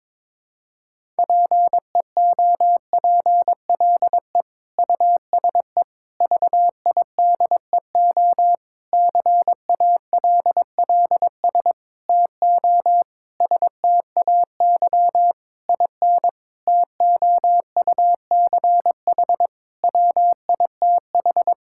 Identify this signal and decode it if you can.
Morse code, heard as a keyed tone, about 22 wpm, 700 Hz. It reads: PEOPLE USE VIDEO CALLS TO STAY IN TOUCH WITH
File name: Difficult Sentences with Hints - 22wpm